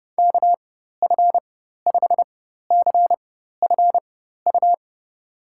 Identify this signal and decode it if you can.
Morse code, heard as a keyed tone, about 30 wpm, spaced as for 14 wpm, 700 Hz. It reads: KF5CFU
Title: USA Call Signs - 6 Characters - 4x Character Spacing - 30wpm